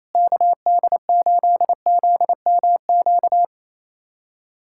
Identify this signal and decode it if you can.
Morse code, heard as a keyed tone, about 28 wpm, 700 Hz. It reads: KD8ZMQ
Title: USA Callsign DE Callsign 28wpm